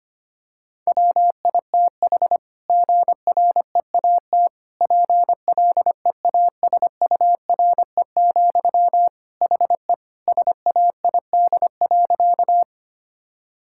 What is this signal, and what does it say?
Morse code, heard as a keyed tone, about 25 wpm, 700 Hz. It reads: WITH GREAT PLEASURE, HE SAID.
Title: Sentences from Top 1600 Words 25wpm